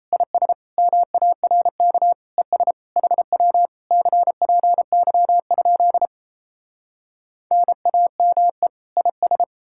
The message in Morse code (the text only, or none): IS MARK ES HW CPY? NAME IS